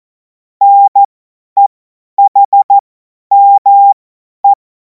Morse code, heard as a keyed tone, about 14 wpm, spaced as for 10 wpm, 800 Hz. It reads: NEHME
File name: Morsekurs / Witze / Witz 31. März 2026 50 F70